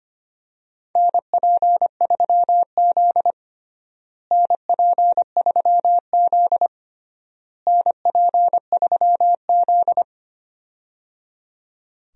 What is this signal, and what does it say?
Morse code, heard as a keyed tone, about 25 wpm, 700 Hz. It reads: NP3Z NP3Z NP3Z
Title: USA Call Signs - 4 Characters - Repeated 3x - 25wpm